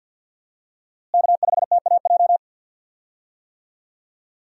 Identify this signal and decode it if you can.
Morse code, heard as a keyed tone, about 50 wpm, 700 Hz. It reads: K5TAJ
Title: USA Call Signs - 5 Characters - Repeated 3x - 50wpm